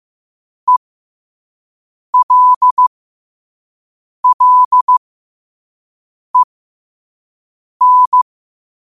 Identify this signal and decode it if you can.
Morse code, heard as a keyed tone, about 15 wpm, spaced as for 5 wpm, 1000 Hz. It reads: ELLEN
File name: qotdfiles / 5wpm